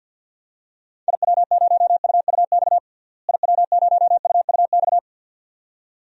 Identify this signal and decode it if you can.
Morse code, heard as a keyed tone, about 53 wpm, 700 Hz. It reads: IW0UUX IW0UUX
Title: DXCC-100 Call Signs - 6 Characters - Repeated 2x - 50wpm